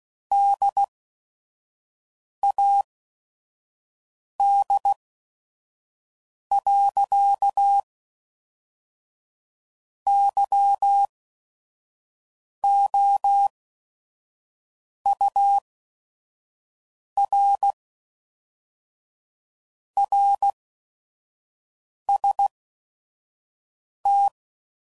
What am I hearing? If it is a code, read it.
Morse code, heard as a keyed tone, about 16 wpm, spaced as for 5 wpm, 800 Hz. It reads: DAD. YOUR RST
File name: CW Code Groups @5WPM / QSOs / QSO WD0FBT